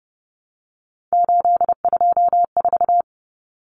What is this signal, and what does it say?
Morse code, heard as a keyed tone, about 30 wpm, 700 Hz. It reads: 824
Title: Numbers - 3-Digit 30wpm